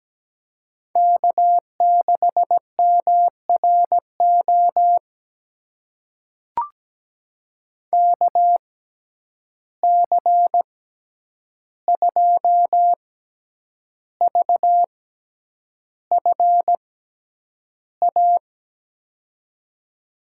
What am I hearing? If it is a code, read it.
Morse code, heard as a keyed tone, about 17 wpm, spaced as for 8 wpm, 700 Hz. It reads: K6MRO KC2VFA
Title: USA Call Signs - 6x Character Spacing - 17wpm